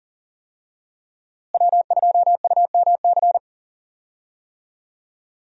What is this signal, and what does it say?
Morse code, heard as a keyed tone, about 40 wpm, 700 Hz. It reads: W2UMC
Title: USA Call Signs - 5 Characters - Repeated 3x - 40wpm